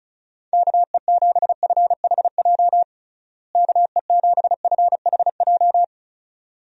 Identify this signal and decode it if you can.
Morse code, heard as a keyed tone, about 34 wpm, 700 Hz. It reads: KE7FHJ KE7FHJ
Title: USA Call Signs - 6 Characters - Repeated 2x - 35wpm